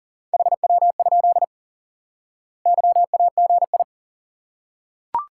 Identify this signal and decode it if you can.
Morse code, heard as a keyed tone, about 40 wpm, 700 Hz. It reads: HW? YAGI
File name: IQE Course 3 - Lesson 89 - YAGI - Two Elements - 40wpm